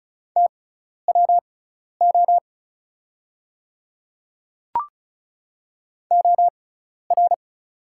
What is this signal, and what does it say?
Morse code, heard as a keyed tone, about 36 wpm, spaced as for 12 wpm, 700 Hz. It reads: TWO OR
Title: 6x Character Spacing - Top 100 Words - 35wpm